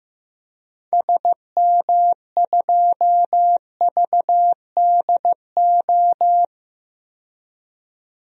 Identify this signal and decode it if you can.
Morse code, heard as a keyed tone, about 15 wpm, 700 Hz. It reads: SM2VDO
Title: DXCC-100 Call Signs - 6 Characters - 15wpm